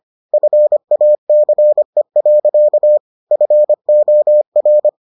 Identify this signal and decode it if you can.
Morse code, heard as a keyed tone, about 25 wpm, 600 Hz. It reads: FACE. FOR